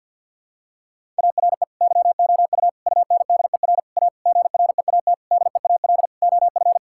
Morse code, heard as a keyed tone, about 50 wpm, 700 Hz. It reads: ARE YOU UNDER A GREAT DEAL OF
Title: Sentences from Top 1000 Words - Review Part 6 - 50wpm